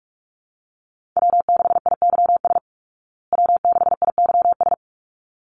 Morse code, heard as a keyed tone, about 44 wpm, 700 Hz. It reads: W6IYS W6IYS
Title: Call Signs Repeated 3x - 45wpm